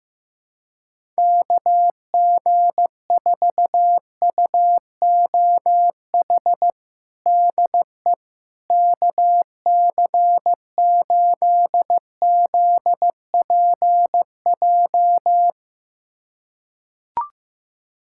Morse code, heard as a keyed tone, about 16 wpm, 700 Hz. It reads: KG4UOH DE KC8ZPJ
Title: USA Callsign DE Callsign - 6 Characters - 15wpm